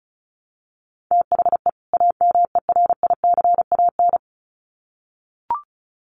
Morse code, heard as a keyed tone, about 34 wpm, 700 Hz. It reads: THE AMERICAN
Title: Sets of 2 Words 35wpm